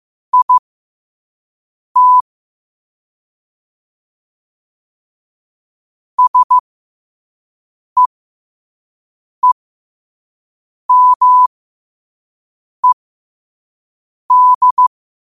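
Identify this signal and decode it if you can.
Morse code, heard as a keyed tone, about 15 wpm, spaced as for 5 wpm, 1000 Hz. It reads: IT SEEMED
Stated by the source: Quote-of-the-day Morse code practice at 5 wpm.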